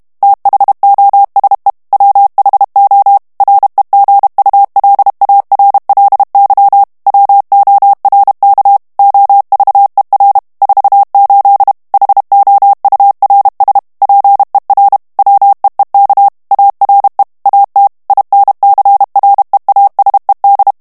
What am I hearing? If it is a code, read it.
Morse code, heard as a keyed tone, about 32 wpm, 800 Hz. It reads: THOSE WHO REGULARLY WORK OVER 48 HOURS PER WEEK ARE AT INCREASED
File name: morse mp3 / ANGLAIS texte clair 1 vitesse 30 mots minute